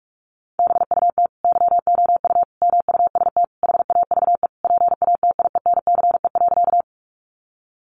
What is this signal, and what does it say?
Morse code, heard as a keyed tone, about 44 wpm, 700 Hz. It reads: BUT YOU MUST HAVE PATIENCE.
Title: Sentences from Top 1300 Words 45wpm